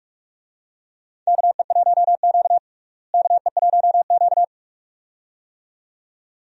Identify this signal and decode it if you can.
Morse code, heard as a keyed tone, about 47 wpm, 700 Hz. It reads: KE1Q KE1Q
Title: USA Call Signs - 4 Characters - Repeated 2x - 45wpm